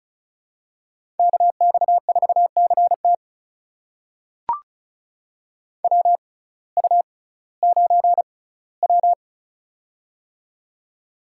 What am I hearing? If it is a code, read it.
Morse code, heard as a keyed tone, about 36 wpm, spaced as for 19 wpm, 700 Hz. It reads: KX4CT WU9W
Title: USA Call Signs - 6x Character Spacing - 35wpm